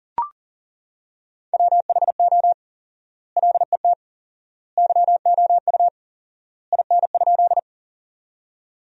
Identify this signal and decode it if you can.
Morse code, heard as a keyed tone, about 40 wpm, 700 Hz. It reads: WHO LET YOU IN?
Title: 4x Word-Spacing - Sentences from Top 300 Words - 40wpm